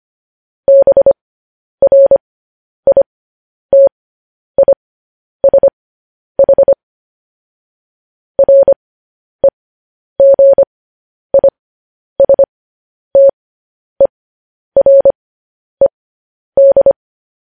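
Morse code, heard as a keyed tone, about 25 wpm, spaced as for 10 wpm, 550 Hz. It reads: BRITISH REGISTERED
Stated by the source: Morse code news headlines sent at 10 wpm.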